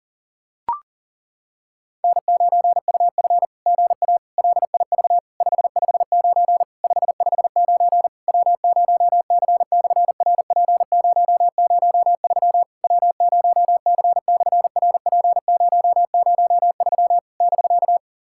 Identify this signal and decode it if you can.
Morse code, heard as a keyed tone, about 40 wpm, 700 Hz. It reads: N9UF GA LIV 559 559 W0C/RP003 W0C/RP003 <BK>
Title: SOTA QSO Practice 40wpm